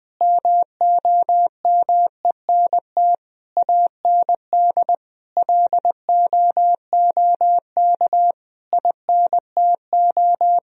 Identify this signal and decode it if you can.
Morse code, heard as a keyed tone, about 20 wpm, 700 Hz. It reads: MOMENT AND LOOK INTO
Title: Sentences from Top 1000 Words - Review Part 1 - 20wpm